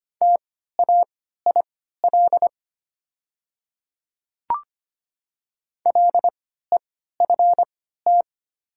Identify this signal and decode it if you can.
Morse code, heard as a keyed tone, about 25 wpm, spaced as for 14 wpm, 700 Hz. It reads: TAIL LEFT
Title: 3x Character Spacing - Top 500 Words - 25wpm